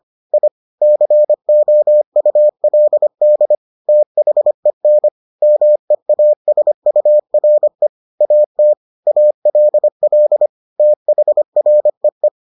Morse code, heard as a keyed tone, about 25 wpm, 600 Hz. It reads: I COULD THEN MEASURE AT ALL THREE